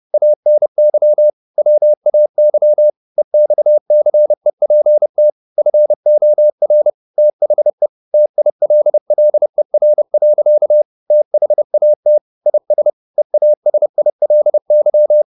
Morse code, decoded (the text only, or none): ANY WAY EXCEPT FOR THE TILLER. THAT IS EASILY